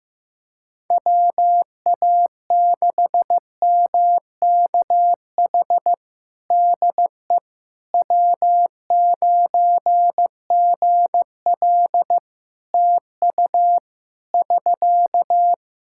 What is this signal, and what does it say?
Morse code, heard as a keyed tone, about 16 wpm, 700 Hz. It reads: WA6MKH DE W9GL TU <SK>
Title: Practice QSO - Exchange 4 - Ending - 15wpm